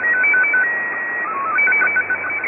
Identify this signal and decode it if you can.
SSTV (slow-scan television), Scottie 1.
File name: items / Httparchive.orgdetailsShortwavePirateRadioCont2 / AmphetamineRadio6925Usb2255z-2337z11-19-15